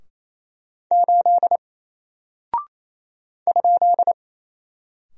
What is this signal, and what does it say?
Morse code, heard as a keyed tone, about 27 wpm, 700 Hz. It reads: 8 ?
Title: Single Character (Letter:Number:Punctuation:Prosign) 28wpm